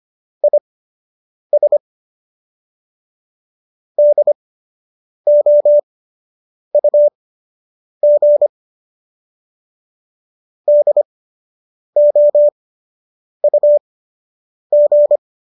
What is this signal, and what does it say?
Morse code, heard as a keyed tone, about 25 wpm, spaced as for 8 wpm, 590 Hz. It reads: IS DOUG DOUG